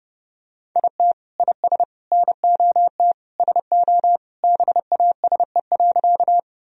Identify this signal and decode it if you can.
Morse code, heard as a keyed tone, about 30 wpm, 700 Hz. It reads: IT IS NOT SO BASE.
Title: Sentences from Top 1000 Words - Review Part 1 - 30wpm